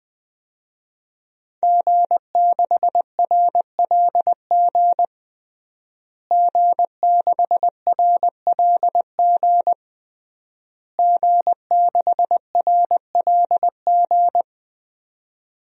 Morse code, heard as a keyed tone, about 20 wpm, 700 Hz. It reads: G6RLG G6RLG G6RLG
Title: DXCC-100 Call Signs - 5 Characters - Repeated 3x - 20wpm